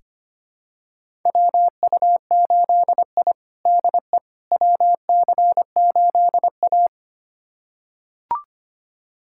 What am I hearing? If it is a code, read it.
Morse code, heard as a keyed tone, about 25 wpm, 700 Hz. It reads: WU8I DE WC8A
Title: USA Callsign DE Callsign - 4 Characters - 25wpm